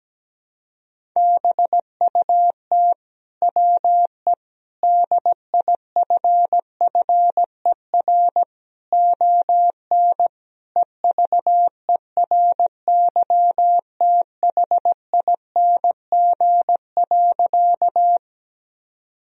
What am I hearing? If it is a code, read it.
Morse code, heard as a keyed tone, about 17 wpm, 700 Hz. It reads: BUT WE DIFFER ON EVERYTHING.
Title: Sentences from Top 1000 Words - Review Part 1 - 17wpm